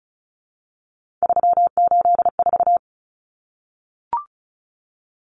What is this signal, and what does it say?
Morse code, heard as a keyed tone, about 34 wpm, 700 Hz. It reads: 384
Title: Common Cut Numbers - 3-Digit 35wpm